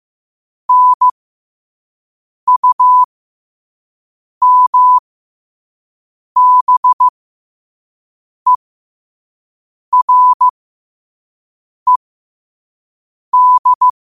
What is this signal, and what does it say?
Morse code, heard as a keyed tone, about 15 wpm, spaced as for 5 wpm, 1000 Hz. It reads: NUMBERED
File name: 5wpm